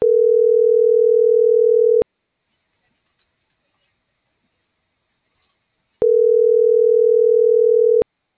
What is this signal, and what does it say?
Ringback tone (North America), 440 + 480 Hz.